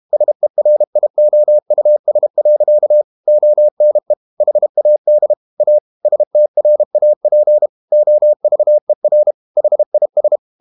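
Morse code, heard as a keyed tone, about 32 wpm, 600 Hz. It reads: SERIOUS. ONE HAD A STRAP OVER HIS
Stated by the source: Morse code practice at 32 wpm from CW med Gnister, reading Ken Follett's novel Never in English.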